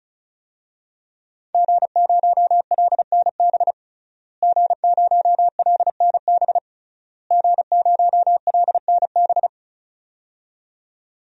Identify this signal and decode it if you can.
Morse code, heard as a keyed tone, about 34 wpm, 700 Hz. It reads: G0LNB G0LNB G0LNB
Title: DXCC-100 Call Signs - 5 Characters - Repeated 3x - 35wpm